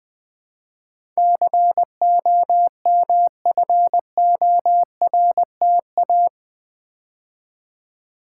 Morse code, heard as a keyed tone, about 20 wpm, 700 Hz. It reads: COMFORTA
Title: Build A Word - 4 Letters At A Time - 11-Letter Words - 20wpm